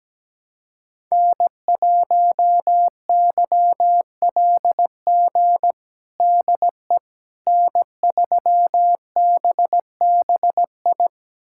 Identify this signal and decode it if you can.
Morse code, heard as a keyed tone, about 17 wpm, 700 Hz. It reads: N1YLG DE N3BBI
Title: USA Callsign DE Callsign - 5 Characters - 17wpm